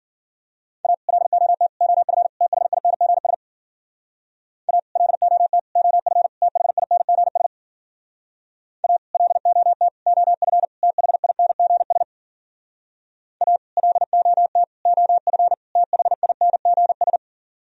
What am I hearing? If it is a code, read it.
Morse code, heard as a keyed tone, about 50 wpm, 700 Hz. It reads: A LOT OF THINGS A LOT OF THINGS A LOT OF THINGS A LOT OF THINGS